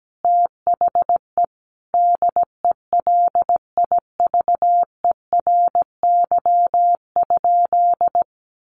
Morse code, heard as a keyed tone, about 17 wpm, 700 Hz. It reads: THE DELIVERY?
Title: Sentences from Top 1500 Words 17wpm